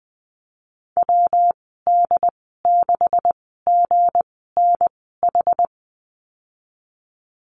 Morse code, heard as a keyed tone, about 20 wpm, spaced as for 15 wpm, 700 Hz. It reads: WD6GNH
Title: 193 Lesson 31g - Letter v - Call Signs 20wpm